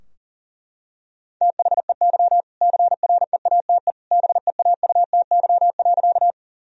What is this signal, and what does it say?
Morse code, heard as a keyed tone, about 40 wpm, 700 Hz. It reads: THEY CREATE BEAUTY.
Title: Sentences from Top 1000 Words - Review Part 3 - 40wpm